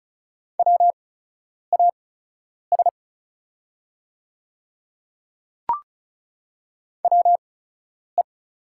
Morse code, heard as a keyed tone, about 34 wpm, spaced as for 10 wpm, 700 Hz. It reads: WAS WE
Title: 8x Character Spacing - Top 100 Words - 35wpm